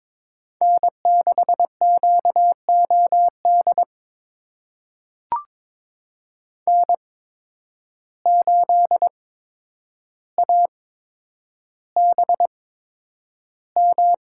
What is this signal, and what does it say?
Morse code, heard as a keyed tone, about 22 wpm, spaced as for 9 wpm, 700 Hz. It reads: N6QOD N8ABM
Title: USA Call Signs - 5 Characters - 8x Character Spacing - 22wpm